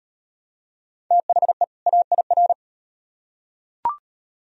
Morse code, heard as a keyed tone, about 37 wpm, 700 Hz. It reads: THE AIR